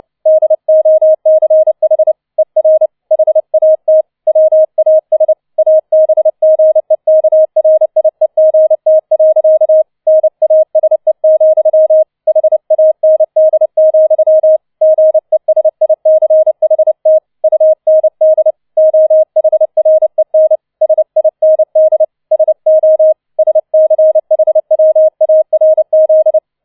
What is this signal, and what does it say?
Morse code, heard as a keyed tone, about 29 wpm, 630 Hz. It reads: DOCH ER HAT WAS ABGEKRIEGT. NASE, HAND, GESICHT UND OHREN SIND SO SCHWARZ